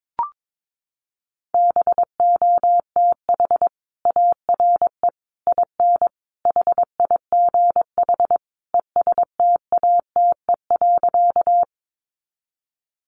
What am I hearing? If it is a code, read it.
Morse code, heard as a keyed tone, about 22 wpm, 700 Hz. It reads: BOTH ARE IN HIGH ESTATE.
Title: Sentences from Top 1200 Words 22wpm